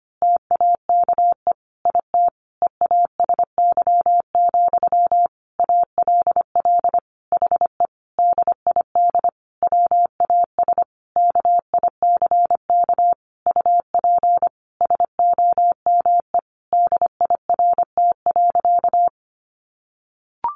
Morse code, heard as a keyed tone, about 25 wpm, 700 Hz. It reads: TAKE IT EASY, ALL HE DID WAS KICK UP SOME DIRT.
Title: A Sound Of Thunder 25wpm